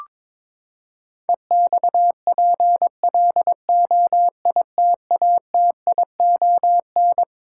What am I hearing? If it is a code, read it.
Morse code, heard as a keyed tone, about 21 wpm, 700 Hz. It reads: EXPLOITATION
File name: practice / 22wpm / Twelve Letter Words - 22wpm